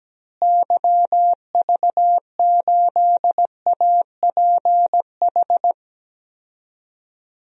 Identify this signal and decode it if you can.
Morse code, heard as a keyed tone, about 17 wpm, 700 Hz. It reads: YV8APH
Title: DXCC-100 Call Signs - 6 Characters - Repeated 3x - 17wpm